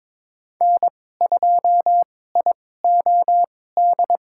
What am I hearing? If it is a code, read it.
Morse code, heard as a keyed tone, about 22 wpm, spaced as for 16 wpm, 700 Hz. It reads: N2IOD
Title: USA Call Signs - 2x Character Spacing - 22wpm